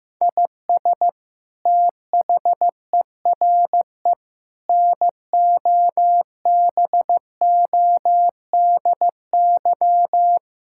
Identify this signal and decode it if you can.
Morse code, heard as a keyed tone, about 15 wpm, 700 Hz. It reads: IS THERE NOBODY
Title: Sentences from Top 1600 Words 15wpm